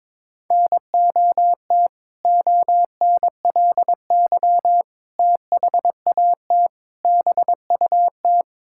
Morse code, heard as a keyed tone, about 22 wpm, 700 Hz. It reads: NOT ONLY THAT BUT
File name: practice / 22wpm / Sentences with 2-Word Phrases - 22wpm